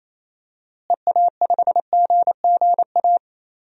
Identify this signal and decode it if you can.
Morse code, heard as a keyed tone, about 28 wpm, 700 Hz. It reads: EA5GGA
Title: DXCC-100 Call Signs - 6 Characters - Repeated 3x - 28wpm